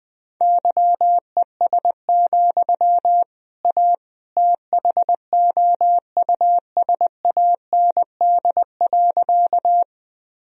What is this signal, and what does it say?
Morse code, heard as a keyed tone, about 20 wpm, 700 Hz. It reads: YES, A THOUSAND.
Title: Sentences from Top 1000 Words - Review Part 3 - 20wpm